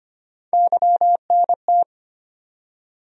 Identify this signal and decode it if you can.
Morse code, heard as a keyed tone, about 25 wpm, 700 Hz. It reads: YNT
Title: USA Call Sign Postfixes - 3 Letters 25wpm